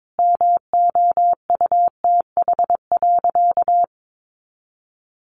Morse code, heard as a keyed tone, about 22 wpm, 700 Hz. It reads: MOUTH.
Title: Sentences from Top 1500 Words 22wpm